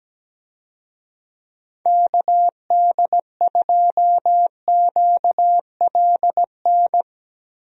Morse code, heard as a keyed tone, about 17 wpm, 700 Hz. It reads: KD2QLN
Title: USA Call Signs - 6 Characters - Repeated 2x - 17wpm